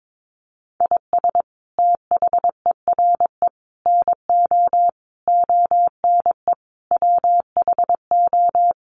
Morse code, heard as a keyed tone, about 22 wpm, 700 Hz. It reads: IS THERE NO ONE WHO
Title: Sentences from Top 200 Words 22wpm